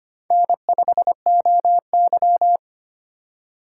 Morse code, heard as a keyed tone, about 25 wpm, 700 Hz. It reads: N5OY
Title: USA Call Signs - 4 Characters - 4x Character Spacing - 25wpm